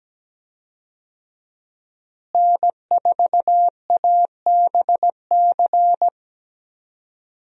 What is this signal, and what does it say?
Morse code, heard as a keyed tone, about 17 wpm, 700 Hz. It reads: N4ABC
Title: USA Call Signs - 5 Characters - Repeated 3x - 17wpm